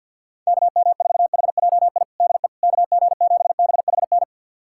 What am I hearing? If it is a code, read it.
Morse code, heard as a keyed tone, about 52 wpm, 690 Hz. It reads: KM4HJI DE KG7BHN
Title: USA Callsign DE Callsign - 6 Characters - 50wpm